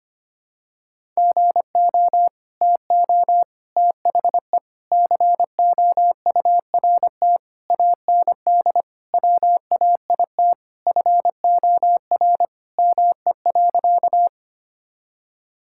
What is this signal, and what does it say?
Morse code, heard as a keyed tone, about 25 wpm, 700 Hz. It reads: GO TO THE COURT AND WAIT FOR ME.
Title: Sentences from Top 1600 Words 25wpm